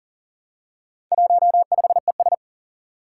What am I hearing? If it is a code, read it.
Morse code, heard as a keyed tone, about 37 wpm, 700 Hz. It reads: 15ES
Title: Four Challenging Letter-Number Combinations 40wpm